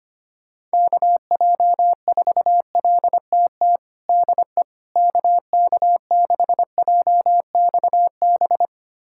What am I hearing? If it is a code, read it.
Morse code, heard as a keyed tone, about 25 wpm, 700 Hz. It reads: KJ4LTT DE KK6JXB
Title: USA Callsign DE Callsign 25wpm